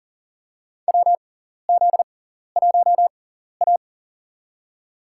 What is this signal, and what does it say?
Morse code, heard as a keyed tone, about 37 wpm, spaced as for 14 wpm, 700 Hz. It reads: WZ1A
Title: USA Call Signs - 4 Characters - 6x Character Spacing - 40wpm